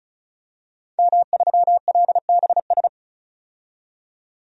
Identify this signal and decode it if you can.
Morse code, heard as a keyed tone, about 36 wpm, 700 Hz. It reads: M3LBS
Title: DXCC-100 Call Signs - 5 Characters - 35wpm